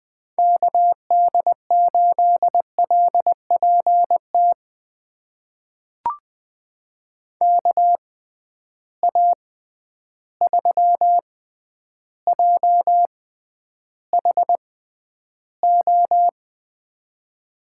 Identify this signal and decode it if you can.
Morse code, heard as a keyed tone, about 20 wpm, spaced as for 10 wpm, 700 Hz. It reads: KD8LPT KA3JHO
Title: USA Call Signs - 6 Characters - 6x Character Spacing - 20wpm